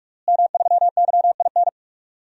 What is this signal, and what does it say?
Morse code, heard as a keyed tone, about 46 wpm, 700 Hz. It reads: M3YIN
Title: DXCC-100 Call Signs - 5 Characters - Repeated 2x - 45wpm